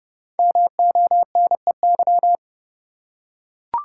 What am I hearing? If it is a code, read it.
Morse code, heard as a keyed tone, about 29 wpm, 700 Hz. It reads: MONEY
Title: Five Letter Words - 30wpm